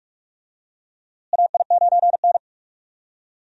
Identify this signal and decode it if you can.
Morse code, heard as a keyed tone, about 46 wpm, 700 Hz. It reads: AI9N
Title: USA Call Signs - 4 Characters - Repeated 3x - 45wpm